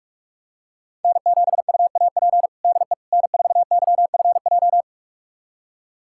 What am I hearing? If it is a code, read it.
Morse code, heard as a keyed tone, about 43 wpm, 700 Hz. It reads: N7UAP DE N4YFJ
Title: USA Callsign DE Callsign 45wpm